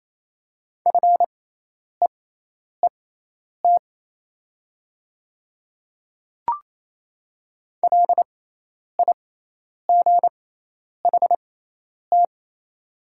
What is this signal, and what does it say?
Morse code, heard as a keyed tone, about 27 wpm, spaced as for 10 wpm, 690 Hz. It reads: FEET LIGHT
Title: 6x Character Spacing - Top 300 Words - 28wpm